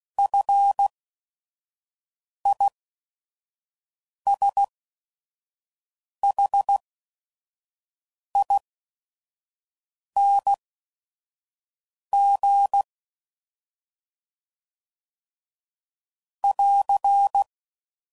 Morse code, heard as a keyed tone, about 16 wpm, spaced as for 5 wpm, 800 Hz. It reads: FISHING <AR>
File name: Mixed / Short Runs / SinglePrefixes